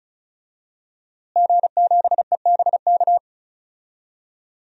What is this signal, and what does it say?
Morse code, heard as a keyed tone, about 36 wpm, 700 Hz. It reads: G7EBK
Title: DXCC-100 Call Signs - 5 Characters - Repeated 2x - 35wpm